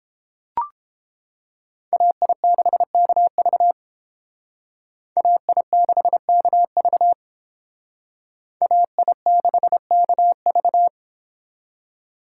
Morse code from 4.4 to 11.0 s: AI6KV AI6KV

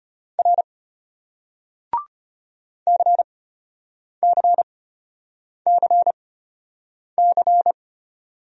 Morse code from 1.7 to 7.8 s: CCCC

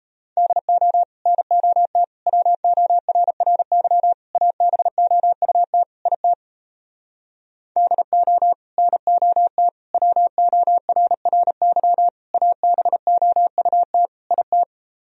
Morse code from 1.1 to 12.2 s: NOT WORRY ABOUT IT DO NOT WORRY

